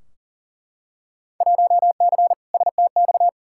Morse code, heard as a keyed tone, about 40 wpm, 700 Hz. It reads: 1C STX
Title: Field Day Exchanges Encore - 40wpm